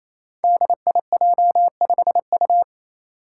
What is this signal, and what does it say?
Morse code, heard as a keyed tone, about 29 wpm, 700 Hz. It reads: DIJ5U
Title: Five Challenging Letter-Number Combinations 28wpm